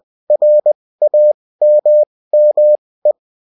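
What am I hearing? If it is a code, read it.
Morse code, heard as a keyed tone, about 20 wpm, spaced as for 16 wpm, 600 Hz. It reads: RAMME
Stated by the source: Bulletin in text and CW form.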